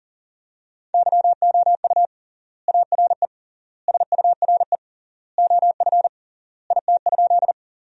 Morse code, heard as a keyed tone, about 40 wpm, 700 Hz. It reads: YOU ARE SURE OF IT?
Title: 3x Word-Spacing - Sentences from Top 300 Words - 40wpm